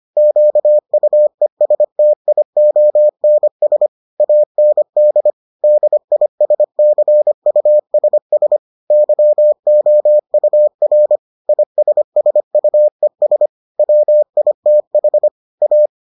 Morse code, heard as a keyed tone, about 25 wpm, 600 Hz. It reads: QUESTIONS AND DISCUSS YOUR ISSUES WITH A